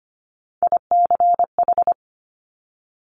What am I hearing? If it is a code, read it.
Morse code, heard as a keyed tone, about 25 wpm, 700 Hz. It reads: ICH
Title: Top Three Letter Combinations in Words 25wpm